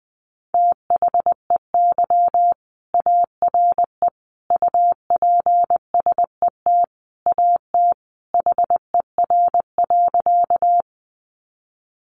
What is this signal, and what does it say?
Morse code, heard as a keyed tone, about 20 wpm, 700 Hz. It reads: THEY ARE UPSET AT HER.
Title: Sentences from Top 800 Words 20wpm